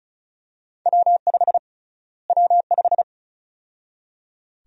Morse code, heard as a keyed tone, about 36 wpm, 700 Hz. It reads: W5 W5
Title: USA Call Sign Prefixes - Repeated 2x - 35wpm